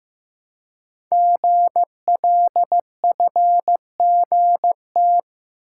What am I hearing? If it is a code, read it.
Morse code, heard as a keyed tone, about 15 wpm, 700 Hz. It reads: GLFGT
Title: Code Groups - 5 Letters - 15wpm